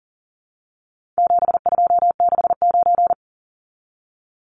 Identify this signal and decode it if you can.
Morse code, heard as a keyed tone, about 40 wpm, 700 Hz. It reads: 7269
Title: Numbers - 4-Digit 40wpm